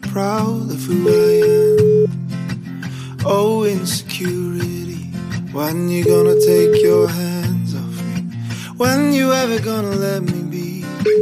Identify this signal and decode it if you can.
Ringback tone (Europe, 425 Hz).